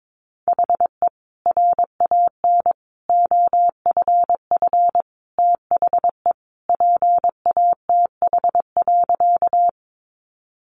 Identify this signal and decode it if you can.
Morse code, heard as a keyed tone, about 22 wpm, 700 Hz. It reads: HE RAN OFF THE PATH.
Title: A Sound Of Thunder 22wpm